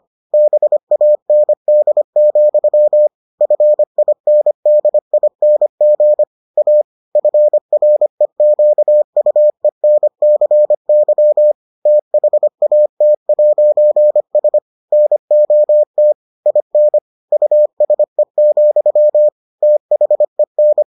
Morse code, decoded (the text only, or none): BAND, FINDING A FREQUENCY THAT'S NOT IN USE, THEN